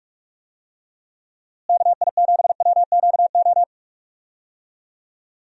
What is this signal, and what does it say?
Morse code, heard as a keyed tone, about 41 wpm, 700 Hz. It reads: KI7WQO
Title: USA Call Signs - 6 Characters - Repeated 3x - 45wpm